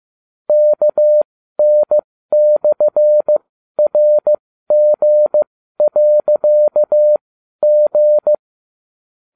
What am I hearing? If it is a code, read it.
Morse code, heard as a keyed tone, about 15 wpm, spaced as for 12 wpm, 600 Hz. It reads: KN/RG.G